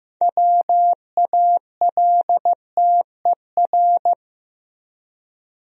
Morse code, heard as a keyed tone, about 15 wpm, 700 Hz. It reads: WALTER
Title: USA Common Names - 15wpm